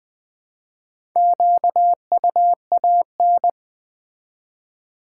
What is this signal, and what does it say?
Morse code, heard as a keyed tone, about 20 wpm, 700 Hz. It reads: QUAN
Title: Build A Word - 2 Letters At A Time - 8-Letter Words - 20wpm